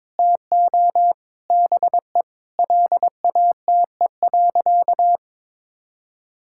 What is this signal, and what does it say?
Morse code, heard as a keyed tone, about 22 wpm, 700 Hz. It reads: TO BE LATE.
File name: practice / 22wpm / Sentences with 4-Word Phrases - 22wpm